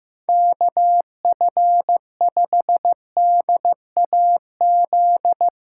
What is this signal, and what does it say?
Morse code, heard as a keyed tone, about 16 wpm, 700 Hz. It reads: KF5DAZ
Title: Speed Racing - USA 2x3 Call Signs 15wpm